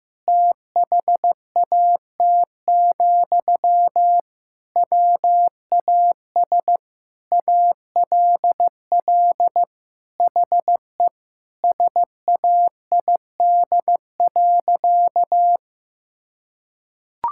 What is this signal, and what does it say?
Morse code, heard as a keyed tone, about 15 wpm, 700 Hz. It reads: THAT, WAS ALL HE SAID.
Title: Sentences from Top 1600 Words 15wpm